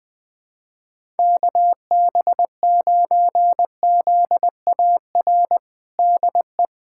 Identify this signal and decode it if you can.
Morse code, heard as a keyed tone, about 20 wpm, 700 Hz. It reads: KB9ZAR DE
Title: USA Callsign DE Callsign - 6 Characters - 20wpm